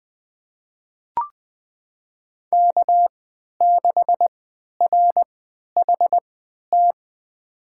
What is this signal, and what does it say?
Morse code, heard as a keyed tone, about 20 wpm, spaced as for 11 wpm, 700 Hz. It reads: K6RHT
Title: USA Call Signs - 5 Characters - 3x Character Spacing - 20wpm